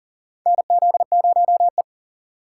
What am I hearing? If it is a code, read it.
Morse code, heard as a keyed tone, about 37 wpm, 700 Hz. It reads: NZ0E
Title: USA Call Signs - 4 Characters - 6x Character Spacing - 40wpm